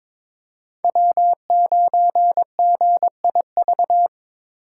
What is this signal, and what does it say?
Morse code, heard as a keyed tone, about 26 wpm, 700 Hz. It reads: W9GIV
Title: Speed Racing - USA Call Signs 22wpm